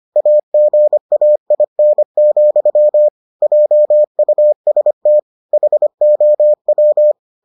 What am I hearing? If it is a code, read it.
Morse code, heard as a keyed tone, about 25 wpm, 600 Hz. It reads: AGAIN, JUST HOW